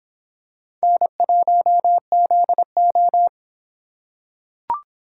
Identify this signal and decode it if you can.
Morse code, heard as a keyed tone, about 26 wpm, 700 Hz. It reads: N1ZO